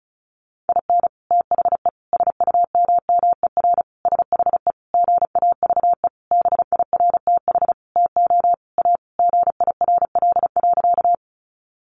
Morse code, heard as a keyed tone, about 34 wpm, 700 Hz. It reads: IN THE SUMMER SHE GAVE BIRTH TO A GIRL.
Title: Sentences from Top 1100 Words 35wpm